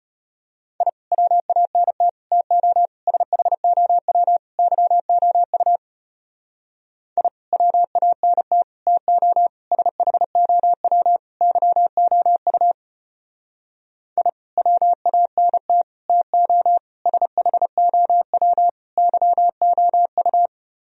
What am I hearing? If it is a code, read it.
Morse code, heard as a keyed tone, about 31 wpm, 700 Hz. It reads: I WANT TO SHOW YOU I WANT TO SHOW YOU I WANT TO SHOW YOU
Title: Speed-Racing - Sets of 5 Words 25wpm